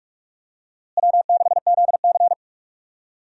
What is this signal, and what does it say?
Morse code, heard as a keyed tone, about 45 wpm, 700 Hz. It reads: W6ZC
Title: USA Call Signs - 4 Characters - Repeated 2x - 45wpm